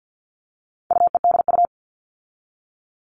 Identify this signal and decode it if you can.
Morse code, heard as a keyed tone, about 50 wpm, 700 Hz. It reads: UEDU
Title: All Cut Numbers - 4-Digit 50wpm